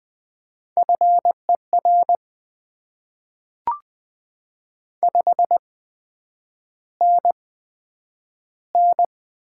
Morse code, heard as a keyed tone, about 20 wpm, spaced as for 9 wpm, 700 Hz. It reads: FER 5NN
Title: Top 20 QSO Elements - 8x Character Spacing - 20wpm